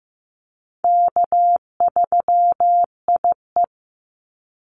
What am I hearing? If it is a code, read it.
Morse code, heard as a keyed tone, about 15 wpm, 700 Hz. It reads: K3IE
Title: Call Signs Repeated 3x - 15wpm